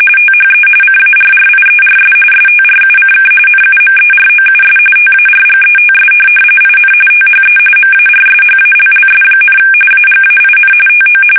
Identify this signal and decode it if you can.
RTTY, about 45.45 baud. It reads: THE CARRIERS STRUCK NUMEROUS TARGETS IN THE MARSHALLS, AGAIN TO
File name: fsk850 / USS-Massachusetts